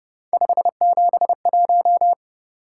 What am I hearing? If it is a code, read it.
Morse code, heard as a keyed tone, about 30 wpm, 700 Hz. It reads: 571